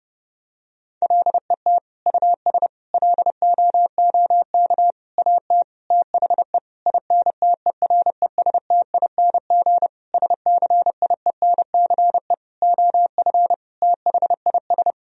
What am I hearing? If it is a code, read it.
Morse code, heard as a keyed tone, about 30 wpm, 700 Hz. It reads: LET US LOOK AT THE INTERESTING SCIENCE OF THIS
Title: Sentences from Top 1000 Words - Review Part 2 - 30wpm